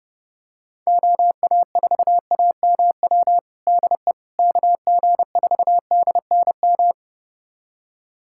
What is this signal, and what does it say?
Morse code, heard as a keyed tone, about 29 wpm, 700 Hz. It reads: OA4AMW DE KG4DNM
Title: USA Callsign DE Callsign 30wpm